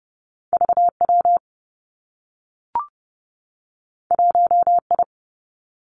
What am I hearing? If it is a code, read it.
Morse code, heard as a keyed tone, about 30 wpm, 700 Hz. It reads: V W 1 I
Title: Two Challenging Letter-Number Combinations 30wpm